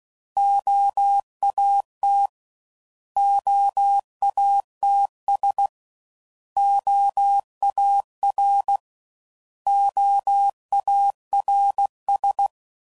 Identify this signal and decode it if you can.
Morse code, heard as a keyed tone, about 16 wpm, 800 Hz. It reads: OAT OATS OAR OARS
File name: Prefixes / By First Letter / Os